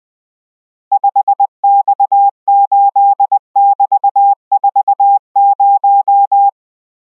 Morse code, heard as a keyed tone, about 20 wpm, 800 Hz. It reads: 5X8<BT>40